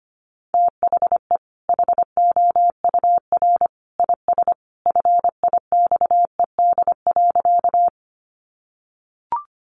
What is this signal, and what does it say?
Morse code, heard as a keyed tone, about 25 wpm, 700 Hz. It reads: THE HOUR IS FIXED.
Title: Sentences from Top 1100 Words 25wpm